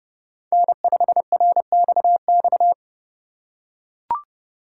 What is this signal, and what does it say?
Morse code, heard as a keyed tone, about 30 wpm, 700 Hz. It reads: N5RXX
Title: USA Call Signs - 5 Characters - 3x Character Spacing - 30wpm